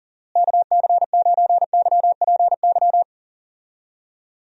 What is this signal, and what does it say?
Morse code, heard as a keyed tone, about 38 wpm, 700 Hz. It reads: KC9YPY
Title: USA Call Signs - 6 Characters - Repeated 2x - 40wpm